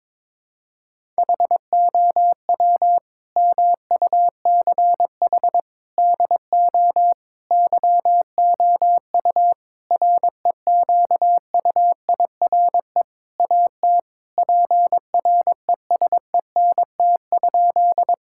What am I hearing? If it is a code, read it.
Morse code, heard as a keyed tone, about 22 wpm, 700 Hz. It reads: HOW MUCH DO YOU REQUIRE AT PRESENT?
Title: Sentences from Top 1000 Words - Review Part 6 - 22wpm